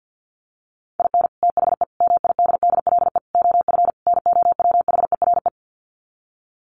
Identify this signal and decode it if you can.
Morse code, heard as a keyed tone, about 48 wpm, 700 Hz. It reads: IN THE MIDDLE OF NOWHERE
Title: Sets of 5 Words Encore 50wpm